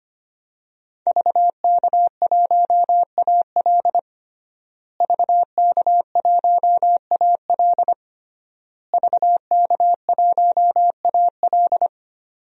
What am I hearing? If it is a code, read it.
Morse code, heard as a keyed tone, about 25 wpm, 700 Hz. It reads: VK1AL VK1AL VK1AL
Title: DXCC-100 Call Signs - 5 Characters - Repeated 3x - 25wpm